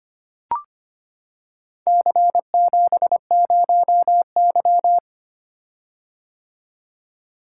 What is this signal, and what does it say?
Morse code, heard as a keyed tone, about 25 wpm, 700 Hz. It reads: C70Y
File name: practice / 25wpm / Four Letter-Number Combinations - 25wpm